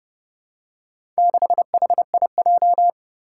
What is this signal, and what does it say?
Morse code, heard as a keyed tone, about 30 wpm, 700 Hz. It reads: 6HIJ
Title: Four Challenging Letter-Number Combinations 30wpm